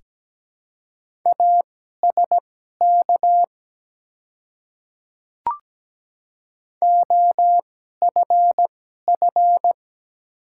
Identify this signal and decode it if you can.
Morse code, heard as a keyed tone, about 17 wpm, spaced as for 12 wpm, 700 Hz. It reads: ASK OFF
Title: 2x Character Spacing - Top 300 Words - 17wpm